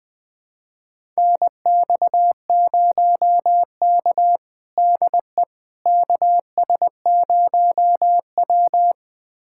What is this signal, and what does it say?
Morse code, heard as a keyed tone, about 21 wpm, 700 Hz. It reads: NX0K DE KS0W
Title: USA Callsign DE Callsign - 4 Characters - 20wpm